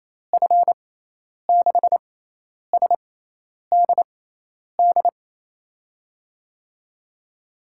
Morse code, heard as a keyed tone, about 28 wpm, spaced as for 10 wpm, 700 Hz. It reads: F6SDD
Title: USA Call Signs - 6 Characters - 6x Character Spacing - 28wpm